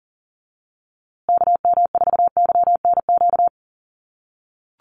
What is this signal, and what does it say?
Morse code, heard as a keyed tone, about 40 wpm, 700 Hz. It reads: KM4YNQ
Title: USA Call Signs Repeated 2x 40wpm